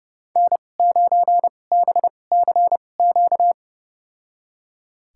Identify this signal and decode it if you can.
Morse code, heard as a keyed tone, about 30 wpm, spaced as for 22 wpm, 700 Hz. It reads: N9BCQ
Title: USA Call Signs - 5 Characters - 2x Character Spacing - 30wpm